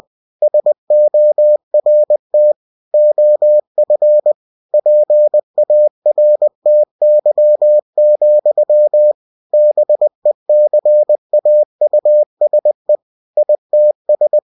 Morse code, heard as a keyed tone, about 20 wpm, 600 Hz. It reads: SORT OF PARTY, BECAUSE ITS
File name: cw-books / winnie-the-pooh / winnie-20wpm-page-0102